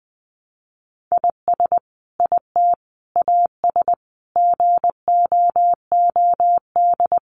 Morse code, heard as a keyed tone, about 20 wpm, 700 Hz. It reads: IS IT AS GOOD
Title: Sentences from Top 200 Words 20wpm